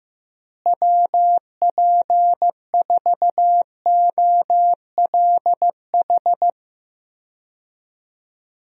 Morse code, heard as a keyed tone, about 15 wpm, 700 Hz. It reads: WP4OLH
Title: USA Call Signs - 6 Characters - Repeated 3x - 15wpm